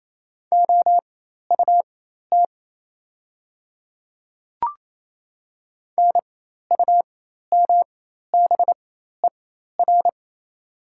Morse code, heard as a keyed tone, about 28 wpm, spaced as for 13 wpm, 690 Hz. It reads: OUT NUMBER
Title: 4x Character Spacing - Top 500 Words - 28wpm